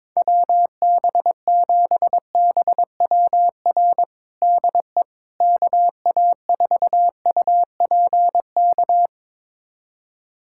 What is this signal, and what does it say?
Morse code, heard as a keyed tone, about 22 wpm, 700 Hz. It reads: WB7BWR DE KA4UPK
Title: USA Callsign DE Callsign 22wpm